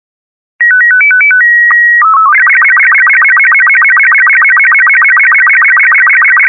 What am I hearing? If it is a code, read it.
SSTV (slow-scan television), Robot 36.